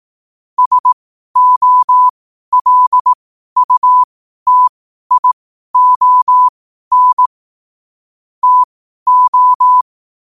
Morse code, heard as a keyed tone, about 18 wpm, spaced as for 12 wpm, 1000 Hz. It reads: SOLUTION TO
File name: qotdfiles / 12wpm